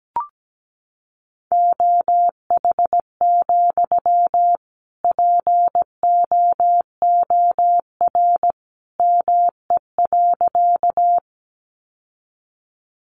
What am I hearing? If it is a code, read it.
Morse code, heard as a keyed tone, about 17 wpm, 700 Hz. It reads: OH, POOR ME.
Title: Sentences from Top 1000 Words 17wpm